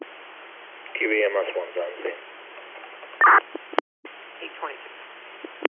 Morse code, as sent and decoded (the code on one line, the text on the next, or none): none